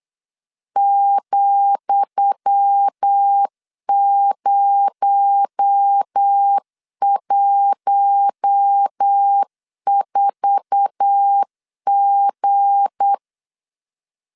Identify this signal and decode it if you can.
Morse code, heard as a keyed tone, about 8 wpm, 800 Hz. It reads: ,014G
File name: LEZIONI / 10 SIGNS / 10D COMA 8wpm 3749